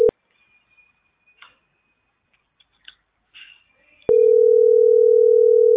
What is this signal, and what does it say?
Ringback tone (North America), 440 + 480 Hz.